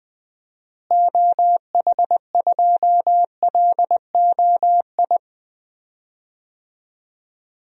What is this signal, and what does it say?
Morse code, heard as a keyed tone, about 20 wpm, 700 Hz. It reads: OH2LOI
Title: DXCC-100 Call Signs - 6 Characters - Repeated 3x - 20wpm